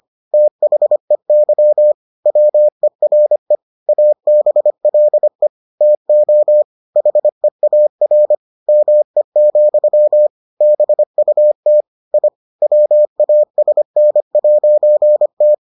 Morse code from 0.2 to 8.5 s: THEY WERE ABLE TO HEAR

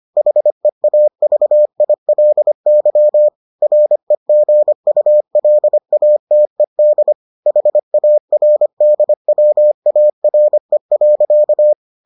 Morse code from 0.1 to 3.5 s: HEAVILY

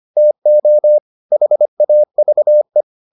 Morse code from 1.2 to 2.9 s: HAVE

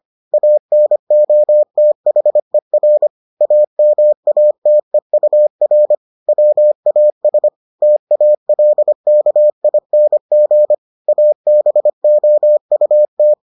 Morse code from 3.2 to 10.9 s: AMATEUR WAS TALKING

